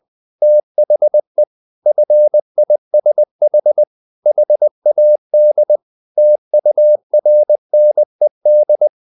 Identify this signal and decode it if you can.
Morse code, heard as a keyed tone, about 20 wpm, 600 Hz. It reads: THE FISH HAD TURNED
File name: the old man and the sea / the old man and the sea part 1 20wpm